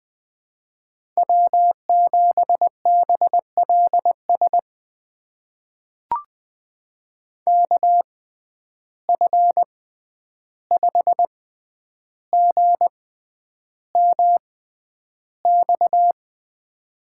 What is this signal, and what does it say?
Morse code, heard as a keyed tone, about 20 wpm, spaced as for 10 wpm, 700 Hz. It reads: W7BLS KF5GMX
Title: USA Call Signs - 6x Character Spacing - 20wpm